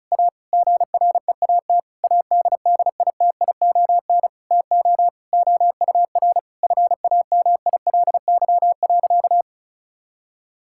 Morse code, heard as a keyed tone, about 34 wpm, 700 Hz. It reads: A GREAT ADDITION TO OUR FAMILY.
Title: Sentences from Top 1000 Words - Review Part 6 - 35wpm